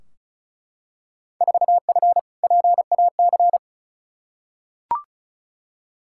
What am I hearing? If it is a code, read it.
Morse code, heard as a keyed tone, about 36 wpm, 700 Hz. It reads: 4F PAC
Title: Field Day Exchanges Encore - 35wpm